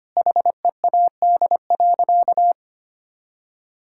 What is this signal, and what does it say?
Morse code, heard as a keyed tone, about 25 wpm, 700 Hz. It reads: HEAD.
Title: Sentences from Top 1600 Words 25wpm